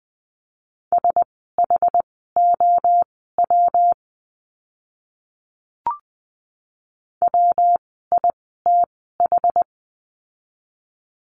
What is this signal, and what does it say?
Morse code, heard as a keyed tone, about 20 wpm, spaced as for 14 wpm, 700 Hz. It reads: SHOW WITH
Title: 206 Lesson 33f - Letter j - Words 20wpm